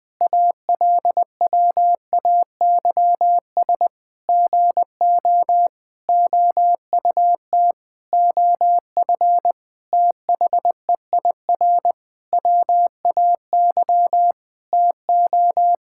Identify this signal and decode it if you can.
Morse code, heard as a keyed tone, about 20 wpm, 700 Hz. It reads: ALWAYS GO OUT OF THEIR WAY TO
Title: Sentences with 6-Word Phrases - 20wpm